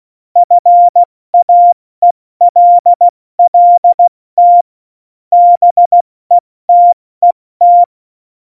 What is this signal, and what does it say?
Morse code, heard as a keyed tone, about 16 wpm, spaced as for 14 wpm, 700 Hz. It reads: FAELLT BETET
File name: Morsekurs / Witze / Witz 09. Oktober 2018, 70 F80